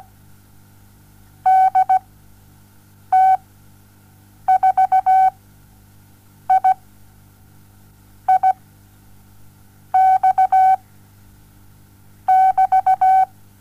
Morse code, heard as a keyed tone, about 16 wpm, spaced as for 6 wpm, 760 Hz. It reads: DT4I I X <BT>
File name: Onlinelehrgang / morsen / DJ4UF Morsen07